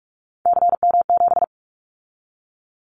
Morse code, heard as a keyed tone, about 46 wpm, 700 Hz. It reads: CM7
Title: Callsign Prefixes for Top 100 DXCC - 45wpm